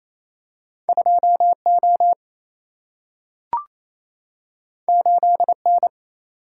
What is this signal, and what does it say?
Morse code, heard as a keyed tone, about 28 wpm, 700 Hz. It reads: 2 O 8 N
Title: Single-Letter Single-Number 28wpm